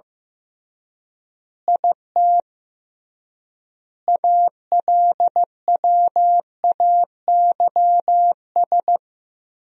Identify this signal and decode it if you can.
Morse code, heard as a keyed tone, about 15 wpm, 700 Hz. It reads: IT ALWAYS
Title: 3x Word-Spacing - Sentences from Top 300 Words - 15wpm